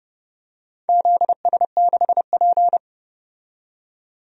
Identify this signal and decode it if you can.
Morse code, heard as a keyed tone, about 30 wpm, 700 Hz. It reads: ZS6P
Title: DXCC-100 Call Signs - 4 Characters - 30wpm